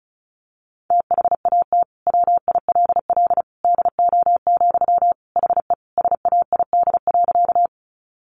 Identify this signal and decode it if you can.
Morse code, heard as a keyed tone, about 34 wpm, 700 Hz. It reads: THAT WILL DO, HE SAID.
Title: Sentences from Top 100 Words 35wpm